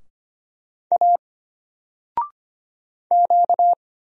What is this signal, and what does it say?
Morse code, heard as a keyed tone, about 25 wpm, 700 Hz. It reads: A Q
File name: Single-Letter-Number-25wpm